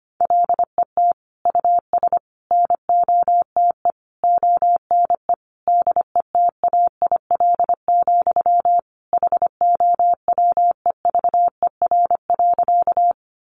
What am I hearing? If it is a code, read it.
Morse code, heard as a keyed tone, about 25 wpm, 700 Hz. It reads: LET US NOTE ONE DETAIL, HOWEVER.
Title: Sentences from Top 1500 Words 25wpm